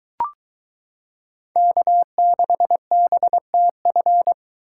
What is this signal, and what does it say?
Morse code, heard as a keyed tone, about 23 wpm, 700 Hz. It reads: K6BTF